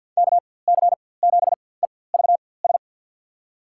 Morse code, heard as a keyed tone, about 49 wpm, spaced as for 23 wpm, 700 Hz. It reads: KC7EVS
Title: USA Call Signs - 6 Characters - 4x Character Spacing - 50wpm